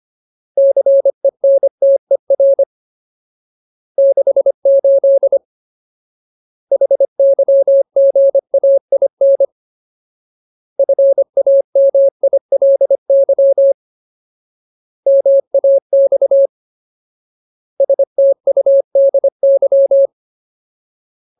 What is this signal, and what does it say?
Morse code, heard as a keyed tone, about 25 wpm, 550 Hz. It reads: CENTER 68 HYGAIN FAMILY MAX STUDY
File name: Ultimate Word List / 25WPM Ultimate